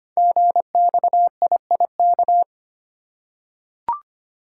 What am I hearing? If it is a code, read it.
Morse code, heard as a keyed tone, about 25 wpm, 700 Hz. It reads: GXIIK
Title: Code Groups - 5 Letters - 25wpm